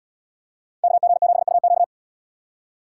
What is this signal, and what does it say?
Morse code, heard as a keyed tone, about 77 wpm, 700 Hz. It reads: KG7UQ